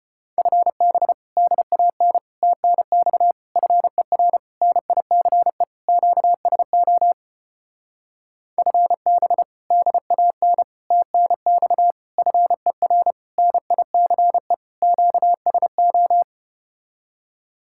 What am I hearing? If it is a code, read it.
Morse code, heard as a keyed tone, about 32 wpm, 700 Hz. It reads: FB DAN TNX FER NICE QSO FB DAN TNX FER NICE QSO